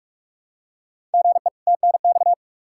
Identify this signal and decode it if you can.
Morse code, heard as a keyed tone, about 44 wpm, 700 Hz. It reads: GE TNX
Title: Practice QSO - Exchange 1 - Beginning - 45wpm